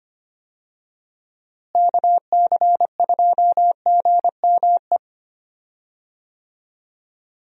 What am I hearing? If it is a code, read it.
Morse code, heard as a keyed tone, about 25 wpm, 700 Hz. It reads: KC2GME
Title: USA Call Signs - 6 Characters - Repeated 2x - 25wpm